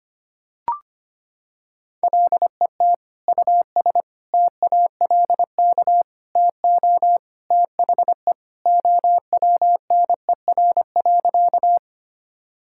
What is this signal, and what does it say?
Morse code, heard as a keyed tone, about 25 wpm, 700 Hz. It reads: LET US TALK TO THE OWNER.
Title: Sentences from Top 1600 Words 25wpm